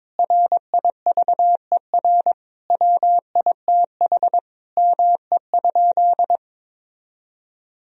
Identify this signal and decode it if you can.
Morse code, heard as a keyed tone, about 22 wpm, 700 Hz. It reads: RIVER WITH ME?
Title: Sentences from Top 1000 Words - Review Part 3 - 22wpm